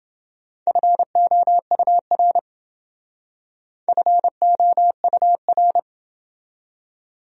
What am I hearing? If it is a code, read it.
Morse code, heard as a keyed tone, about 29 wpm, 700 Hz. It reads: FOUR FOUR